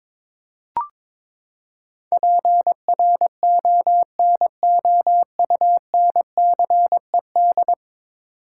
Morse code, heard as a keyed tone, about 22 wpm, 700 Hz. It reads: PRONOUNCED
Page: morse-code-ninja-practice-22wpm